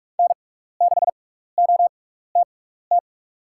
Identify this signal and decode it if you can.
Morse code, heard as a keyed tone, about 44 wpm, spaced as for 15 wpm, 700 Hz. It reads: N6OTT
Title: USA Call Signs - 5 Characters - 6x Character Spacing - 45wpm